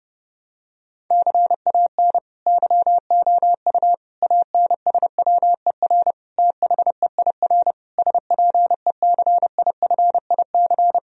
Morse code, heard as a keyed tone, about 30 wpm, 700 Hz. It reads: CAN YOU ANSWER THEIR SPECIFIC
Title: Sentences from Top 1000 Words - Review Part 3 - 30wpm